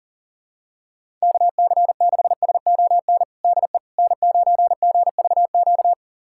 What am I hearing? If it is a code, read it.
Morse code, heard as a keyed tone, about 40 wpm, 700 Hz. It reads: KC6SON DE N9GVQ
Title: USA Callsign DE Callsign 40wpm